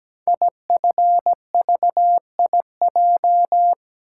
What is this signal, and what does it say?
Morse code, heard as a keyed tone, about 17 wpm, 700 Hz. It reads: IFVIJ
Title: Code Groups - 5 Letters - 17wpm